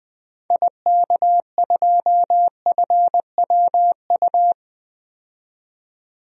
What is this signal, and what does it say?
Morse code, heard as a keyed tone, about 20 wpm, 700 Hz. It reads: IK2FWU
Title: DXCC-100 Call Signs - 6 Characters - 20wpm